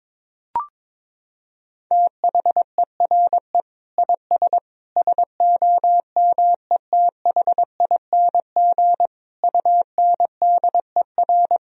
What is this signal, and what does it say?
Morse code, heard as a keyed tone, about 22 wpm, 700 Hz. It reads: THERE IS SOMETHING UNDER
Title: Sentences from Top 1000 Words - Review Part 1 - 22wpm